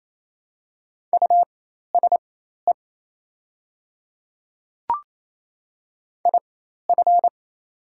Morse code, heard as a keyed tone, about 28 wpm, spaced as for 13 wpm, 700 Hz. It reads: USE IF
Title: 4x Character Spacing - Top 100 Words - 28wpm